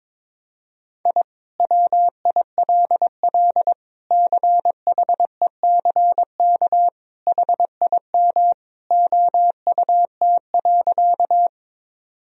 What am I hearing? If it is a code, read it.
Morse code, heard as a keyed tone, about 22 wpm, 700 Hz. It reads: I WILL CHECK HIM OUT.
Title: Sentences from Top 1000 Words - Review Part 3 - 22wpm